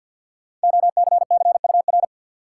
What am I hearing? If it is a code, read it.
Morse code, heard as a keyed tone, about 52 wpm, 700 Hz. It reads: OCCUR
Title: Sentences from Top 900 Words - New Word Drill - 50wpm